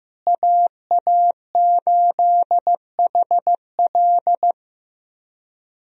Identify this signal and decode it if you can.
Morse code, heard as a keyed tone, about 15 wpm, 700 Hz. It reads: AA8HL
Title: USA Call Signs - 5 Characters - Repeated 3x - 15wpm